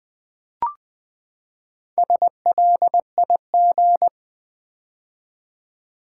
Morse code, heard as a keyed tone, about 20 wpm, 700 Hz. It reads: SLIG
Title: Build A Word - 4 Letters At A Time - 8-Letter Words - 20wpm